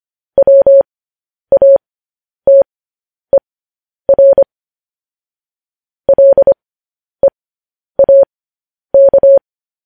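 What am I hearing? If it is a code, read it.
Morse code, heard as a keyed tone, about 25 wpm, spaced as for 10 wpm, 550 Hz. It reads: WATER LEAK
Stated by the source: Morse code news headlines broadcast.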